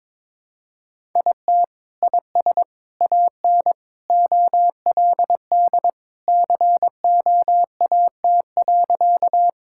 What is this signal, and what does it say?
Morse code, heard as a keyed tone, about 22 wpm, 700 Hz. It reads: IT IS AN OLD COAT.
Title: Sentences from Top 1600 Words 22wpm